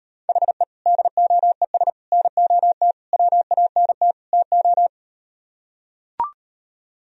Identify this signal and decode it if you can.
Morse code, heard as a keyed tone, about 38 wpm, 700 Hz. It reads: HE DOES NOT WANT TO